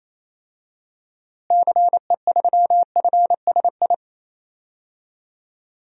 Morse code, heard as a keyed tone, about 28 wpm, 700 Hz. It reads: CE3FSI
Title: DXCC-100 Call Signs - 6 Characters - Repeated 2x - 28wpm